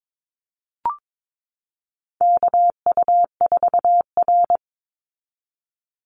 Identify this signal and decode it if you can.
Morse code, heard as a keyed tone, about 22 wpm, 700 Hz. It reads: KU4R
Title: NA SOTA Chaser Callsigns 22wpm